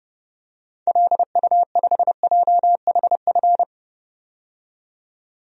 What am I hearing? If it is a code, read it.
Morse code, heard as a keyed tone, about 30 wpm, 700 Hz. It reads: LU5JHF
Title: DXCC-100 Call Signs - 6 Characters - 30wpm